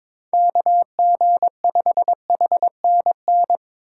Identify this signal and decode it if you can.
Morse code, heard as a keyed tone, about 22 wpm, 700 Hz. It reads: KG5HNN
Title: USA Call Signs - 4x Character Spacing - 22wpm